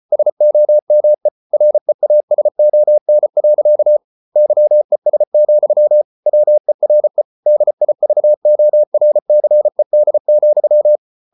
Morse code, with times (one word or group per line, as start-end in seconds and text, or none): 0.1-1.3 SOME
1.5-4.0 REASON.
4.4-6.0 YES,
6.3-7.2 WERE
7.5-11.0 DIVORCED,